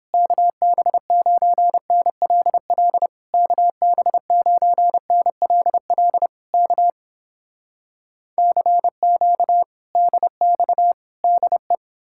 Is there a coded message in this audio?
Yes: Morse code from 0.1 to 11.7 s, heard as a keyed tone, about 31 wpm, 700 Hz. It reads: KB9NLL KB9NLL K CQ DX DE